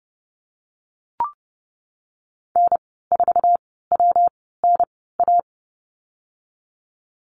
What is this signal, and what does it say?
Morse code, heard as a keyed tone, about 30 wpm, spaced as for 17 wpm, 700 Hz. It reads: N4WNA
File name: course / 30:10wpm / 116 Lesson 19g - Letter w - Call Signs 30wpm